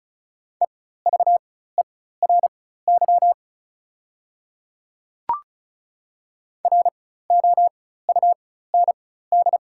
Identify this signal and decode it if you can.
Morse code, heard as a keyed tone, about 33 wpm, spaced as for 16 wpm, 690 Hz. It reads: EVERY ROUND
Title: 4x Character Spacing - Top 200 Words - 35wpm